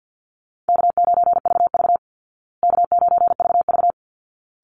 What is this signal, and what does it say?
Morse code, heard as a keyed tone, about 48 wpm, 700 Hz. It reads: K9VV K9VV
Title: Call Signs Repeated 3x - 50wpm